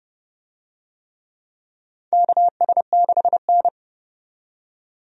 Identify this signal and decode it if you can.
Morse code, heard as a keyed tone, about 29 wpm, 700 Hz. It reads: KS6N
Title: USA Call Signs - 4 Characters - 30wpm